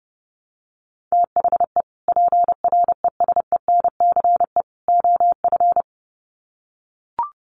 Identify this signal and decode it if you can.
Morse code, heard as a keyed tone, about 30 wpm, 700 Hz. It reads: THE PRESENCE OF
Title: Sets of 3 Words 30wpm